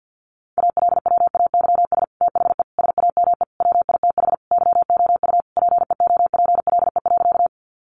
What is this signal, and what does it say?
Morse code, heard as a keyed tone, about 48 wpm, 700 Hz. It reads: ALWAYS THE SAME WITH YOU PEOPLE.
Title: Sentences from Top 300 Words 50wpm